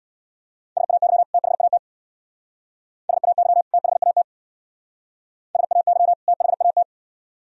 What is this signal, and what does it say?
Morse code, heard as a keyed tone, about 57 wpm, 700 Hz. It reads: SAY THAT SAY THAT SAY THAT
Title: Speed-Racing - Sets of 2 Words 50wpm